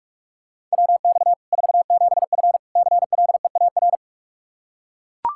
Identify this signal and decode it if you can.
Morse code, heard as a keyed tone, about 41 wpm, 700 Hz. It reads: WX 47F CLEAR
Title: Practice QSO - Exchange 2 - Info - 45wpm